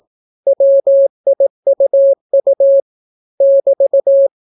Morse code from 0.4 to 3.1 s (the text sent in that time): WIUU